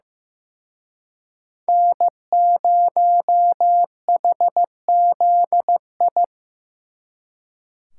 Morse code, heard as a keyed tone, about 18 wpm, 700 Hz. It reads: N0HZI
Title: Speed Racing - USA 1x3 Call Signs 15wpm